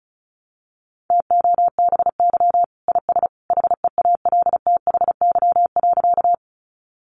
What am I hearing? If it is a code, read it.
Morse code, heard as a keyed tone, about 34 wpm, 700 Hz. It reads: TOBY IS HEALTHY.
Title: Easy Phrases 35wpm